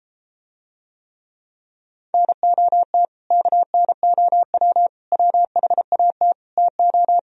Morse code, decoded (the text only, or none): NOT KNOW WHAT TO